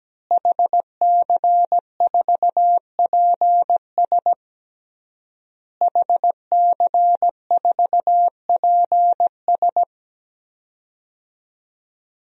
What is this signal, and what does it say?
Morse code, heard as a keyed tone, about 17 wpm, 700 Hz. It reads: HC4PS HC4PS
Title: DXCC-100 Call Signs - 5 Characters - Repeated 3x - 17wpm